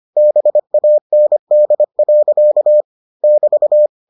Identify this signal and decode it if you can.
Morse code, heard as a keyed tone, about 25 wpm, 600 Hz. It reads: BAND. <BT>